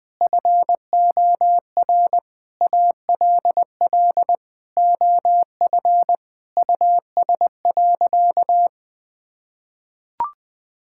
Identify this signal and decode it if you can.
Morse code, heard as a keyed tone, about 20 wpm, 700 Hz. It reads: FOR ALL OF US.
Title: Sentences with 4-Word Phrases - 20wpm